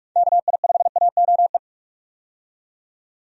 Morse code, heard as a keyed tone, about 46 wpm, 700 Hz. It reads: KI5AOE
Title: USA Call Signs - 6 Characters - 45wpm